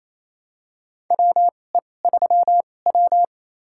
Morse code, heard as a keyed tone, about 28 wpm, spaced as for 20 wpm, 700 Hz. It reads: WE3W
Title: USA Call Signs - 4 Characters - 2x Character Spacing - 28wpm